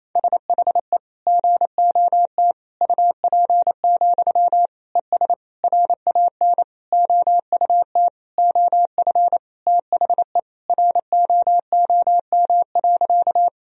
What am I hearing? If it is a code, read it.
Morse code, heard as a keyed tone, about 27 wpm, 700 Hz. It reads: SHE GOT UP, ES RAN OUT OF THE ROOM.
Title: Sentences from Top 500 Words and Abbreviations 28wpm